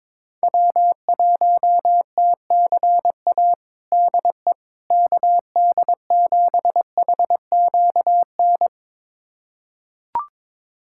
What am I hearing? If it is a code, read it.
Morse code, heard as a keyed tone, about 22 wpm, 700 Hz. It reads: W1TCA DE KD7HQN
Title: USA Callsign DE Callsign 22wpm